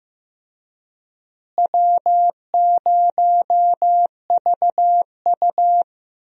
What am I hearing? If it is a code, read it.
Morse code, heard as a keyed tone, about 15 wpm, 700 Hz. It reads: W0VU
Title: USA Call Signs - 4 Characters - Repeated 2x - 15wpm